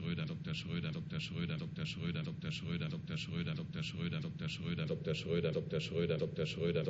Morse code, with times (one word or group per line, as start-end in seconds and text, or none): none